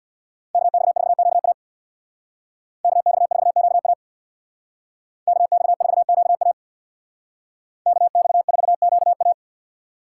Morse code, heard as a keyed tone, about 64 wpm, 700 Hz. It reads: KX4QA KX4QA KX4QA KX4QA